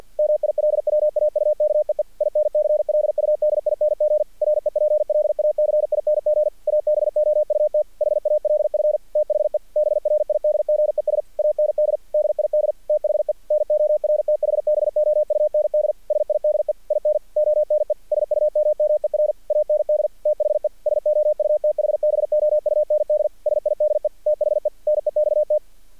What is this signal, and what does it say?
Morse code, heard as a keyed tone, about 48 wpm, 600 Hz. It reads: MILWAUKEE INCLUDING REPLACING ABOUT HALF THE BRIDGES AND DID THE NORTHBOUND SIDE IN ONE SUMMER AND THE SOUTHBOUND SIDE THE NEXT
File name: archived / CRSnet